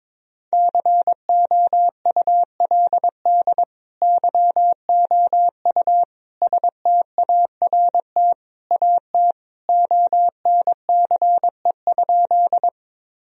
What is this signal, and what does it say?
Morse code, heard as a keyed tone, about 21 wpm, 700 Hz. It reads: COULD YOU START AT ONCE?
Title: Sentences from Top 1000 Words - Review Part 1 - 22wpm